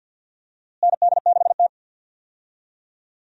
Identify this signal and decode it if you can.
Morse code, heard as a keyed tone, about 50 wpm, 700 Hz. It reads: ND6T
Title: USA Call Signs - 4 Characters - Repeated 2x - 50wpm